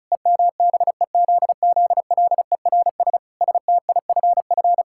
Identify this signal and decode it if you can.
Morse code, heard as a keyed tone, about 36 wpm, 700 Hz. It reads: EMBEZZLERS STIFF
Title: Pangram Challenge v2 - 35wpm